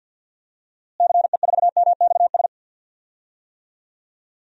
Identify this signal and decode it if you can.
Morse code, heard as a keyed tone, about 50 wpm, 700 Hz. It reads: CE4MXS
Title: DXCC-100 Call Signs - 6 Characters - Repeated 3x - 50wpm